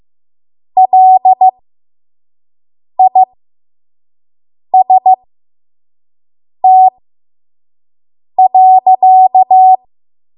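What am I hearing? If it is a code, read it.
Morse code, heard as a keyed tone, about 15 wpm, spaced as for 5 wpm, 750 Hz. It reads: LIST.
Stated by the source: 5 WPM Morse code practice.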